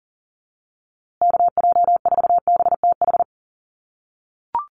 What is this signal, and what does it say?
Morse code, heard as a keyed tone, about 39 wpm, 700 Hz. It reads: KJ4BTH
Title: USA Call Signs Repeated 3x 40wpm